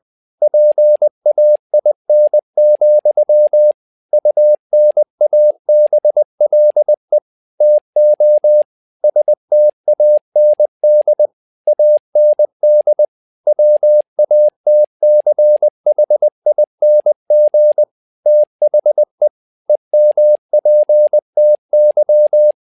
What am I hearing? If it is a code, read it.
Morse code, heard as a keyed tone, about 20 wpm, 600 Hz. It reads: PAIN, UNABLE TO STAND AND WATCHING THE EMPTY
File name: tower trouble 20wpm